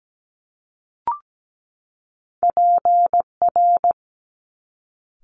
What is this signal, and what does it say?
Morse code, heard as a keyed tone, about 17 wpm, 700 Hz. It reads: PR
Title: Top Two Letter Combinations in Words 17wpm